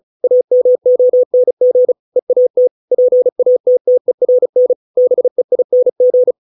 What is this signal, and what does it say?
Morse code, heard as a keyed tone, about 34 wpm, 500 Hz. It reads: AMONG EAT PATTERN BEING